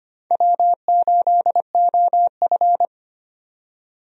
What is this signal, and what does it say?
Morse code, heard as a keyed tone, about 25 wpm, 700 Hz. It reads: W8OF
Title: USA Call Signs - 4 Characters - 25wpm